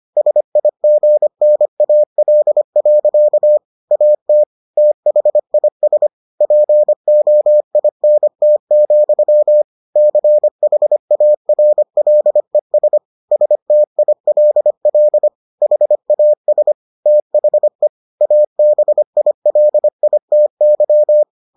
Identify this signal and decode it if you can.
Morse code, heard as a keyed tone, about 25 wpm, 600 Hz. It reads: SIGNAL. AT THIS POINT, CHARLES STILL HAS THE ABILITY